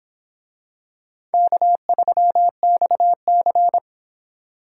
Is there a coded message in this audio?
Yes: Morse code from 1.3 to 3.8 s, heard as a keyed tone, about 26 wpm, 700 Hz. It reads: K3XC